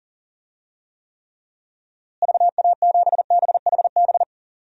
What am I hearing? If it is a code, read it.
Morse code, heard as a keyed tone, about 40 wpm, 700 Hz. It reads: VA7BHB
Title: DXCC-100 Call Signs - 6 Characters - Repeated 3x - 40wpm